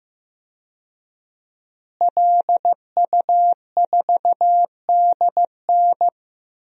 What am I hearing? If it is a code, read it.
Morse code, heard as a keyed tone, about 15 wpm, 700 Hz. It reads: LU4DN
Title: DXCC-100 Call Signs - 5 Characters - Repeated 3x - 15wpm